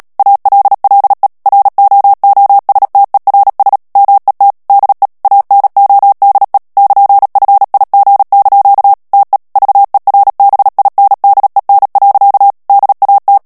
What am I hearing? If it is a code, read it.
Morse code, heard as a keyed tone, about 37 wpm, 800 Hz. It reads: ALLE ROOSTERS MET DE ANODE <KN>FIG) TE VERBINDEN. DAT
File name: morse mp3 / HOLLANDAIS texte clair 2 vitesse 35 mots minute